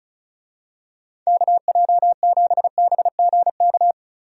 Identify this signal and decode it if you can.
Morse code, heard as a keyed tone, about 36 wpm, 700 Hz. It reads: KJ7BGK
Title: USA Call Signs - 6 Characters - 35wpm